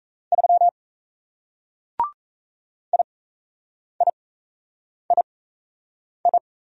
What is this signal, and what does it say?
Morse code, heard as a keyed tone, about 38 wpm, spaced as for 8 wpm, 700 Hz. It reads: 3 IIII